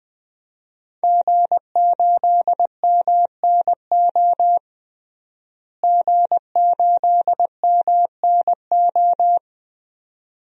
Morse code, heard as a keyed tone, about 21 wpm, 700 Hz. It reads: G8MNO G8MNO
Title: DXCC-100 Call Signs - 5 Characters - Repeated 2x - 20wpm